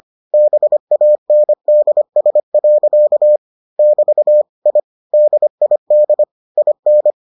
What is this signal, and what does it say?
Morse code, heard as a keyed tone, about 25 wpm, 600 Hz. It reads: BANDS. <BT> I DID IN